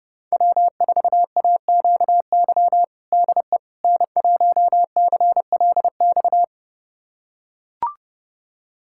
Morse code, heard as a keyed tone, about 30 wpm, 700 Hz. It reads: W4AQY DE N1CLX
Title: USA Callsign DE Callsign - 5 Characters - 30wpm